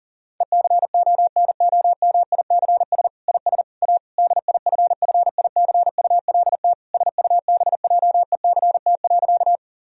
Morse code, heard as a keyed tone, about 37 wpm, 700 Hz. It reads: ECONOMICS IS A DIFFICULT SUBJECT.
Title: Sentences from Top 1000 Words - Review Part 3 - 40wpm